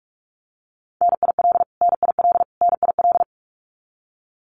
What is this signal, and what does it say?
Morse code, heard as a keyed tone, about 44 wpm, 700 Hz. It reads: NIL NIL NIL
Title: Abbreviations Repeated 3x - 45wpm